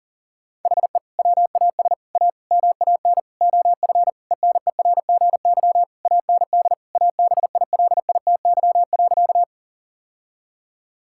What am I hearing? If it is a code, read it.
Morse code, heard as a keyed tone, about 37 wpm, 700 Hz. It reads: HE WAS A MAN OF ENERGY AND ABILITY.
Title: Sentences from Top 1000 Words - Review Part 6 - 40wpm